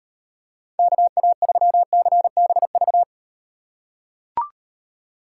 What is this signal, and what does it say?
Morse code, heard as a keyed tone, about 36 wpm, 700 Hz. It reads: KA3CBV